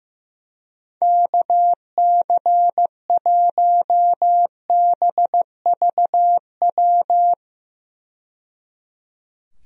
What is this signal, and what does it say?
Morse code, heard as a keyed tone, about 15 wpm, 700 Hz. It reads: KC1BVW
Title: USA Call Signs - 6 Characters - 15wpm